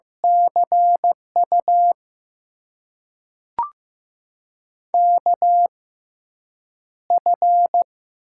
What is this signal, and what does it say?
Morse code, heard as a keyed tone, about 15 wpm, spaced as for 8 wpm, 700 Hz. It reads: CU KF
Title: USA Call Signs - 6 Characters - 6x Character Spacing - 15wpm